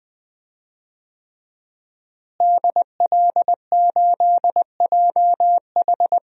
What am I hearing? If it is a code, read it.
Morse code, heard as a keyed tone, about 20 wpm, 700 Hz. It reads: DL8JH
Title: DXCC-100 Call Signs - 5 Characters - Repeated 3x - 20wpm